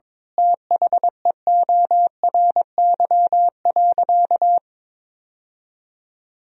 Morse code, heard as a keyed tone, about 21 wpm, 700 Hz. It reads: THEORY.
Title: Sentences from Top 1000 Words - Review Part 4 - 22wpm